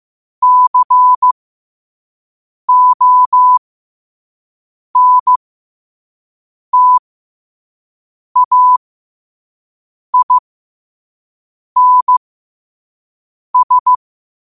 Morse code, heard as a keyed tone, about 15 wpm, spaced as for 5 wpm, 1000 Hz. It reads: CONTAINS
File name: qotdfiles / 5wpm